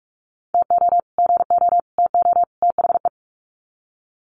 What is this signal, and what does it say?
Morse code, heard as a keyed tone, about 44 wpm, 700 Hz. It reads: TO GO TO THE
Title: Sets of 4 Words 45wpm